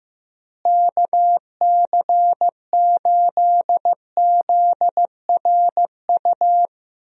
Morse code, heard as a keyed tone, about 15 wpm, 700 Hz. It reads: KC8ZRU
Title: USA Call Signs - 6 Characters - Repeated 3x - 15wpm